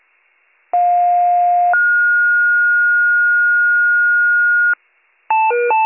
Two-tone page, heard as a one-second tone, then a three-second tone, about 710 then 1430 Hz. It is heard from 0.7 to 4.7 s.